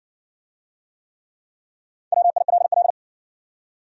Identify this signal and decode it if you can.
Morse code, heard as a keyed tone, about 60 wpm, 700 Hz. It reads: WILL